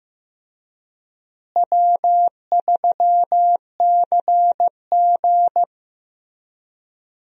Morse code, heard as a keyed tone, about 15 wpm, 700 Hz. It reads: W3CG
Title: USA Call Signs - 4 Characters - Repeated 3x - 15wpm